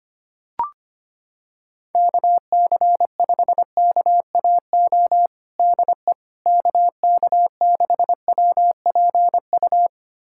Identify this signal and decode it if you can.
Morse code, heard as a keyed tone, about 25 wpm, 700 Hz. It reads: KC5KAO DE KK6WPU
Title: USA Callsign DE Callsign 25wpm